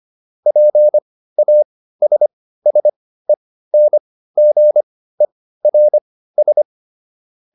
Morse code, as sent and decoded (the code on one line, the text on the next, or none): .--. .- ... ... -. --. .-. ...
PASSENGERS